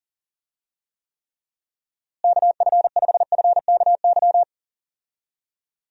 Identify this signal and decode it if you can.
Morse code, heard as a keyed tone, about 40 wpm, 700 Hz. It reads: KF5FKY
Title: USA Call Signs - 6 Characters - Repeated 2x - 40wpm